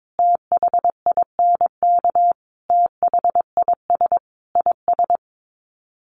Morse code, heard as a keyed tone, about 22 wpm, 700 Hz. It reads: THINK THIS IS
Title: Sets of 5 Words Encore2 22wpm